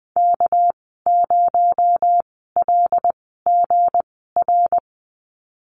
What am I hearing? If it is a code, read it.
Morse code, heard as a keyed tone, about 20 wpm, spaced as for 14 wpm, 700 Hz. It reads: K0LGR
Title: 231 Lesson 37g - Letter x - Call Signs 20wpm